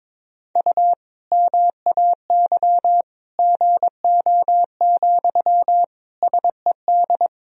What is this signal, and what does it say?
Morse code, heard as a keyed tone, about 22 wpm, 700 Hz. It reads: U MAY GO, SED
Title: Sentences from Top 400 Words and Abbreviations 22wpm